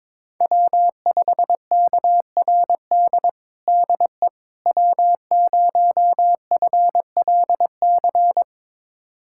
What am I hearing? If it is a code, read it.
Morse code, heard as a keyed tone, about 22 wpm, 700 Hz. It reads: W5KRD DE W0FLC
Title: USA Callsign DE Callsign - 5 Characters - 22wpm